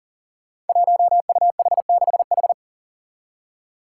Morse code, heard as a keyed tone, about 40 wpm, 700 Hz. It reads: 1UH6H
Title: Five Challenging Letter-Number Combinations 40wpm